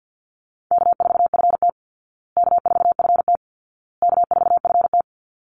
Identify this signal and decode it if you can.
Morse code, heard as a keyed tone, about 48 wpm, 700 Hz. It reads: K4FT K4FT K4FT
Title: Easy Call Signs Repeated 3x - 50wpm